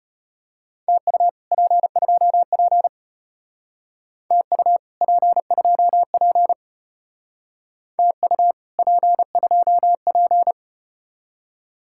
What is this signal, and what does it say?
Morse code, heard as a keyed tone, about 33 wpm, 700 Hz. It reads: TU P2P TU P2P TU P2P